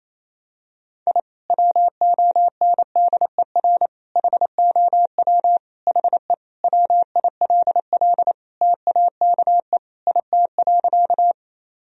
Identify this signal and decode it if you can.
Morse code, heard as a keyed tone, about 28 wpm, 700 Hz. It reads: I WONDER HOW HE WILL TAKE IT.
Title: Sentences from Top 1000 Words - Review Part 3 - 28wpm